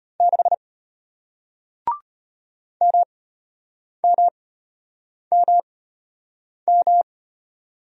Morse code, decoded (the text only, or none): 6 MMMM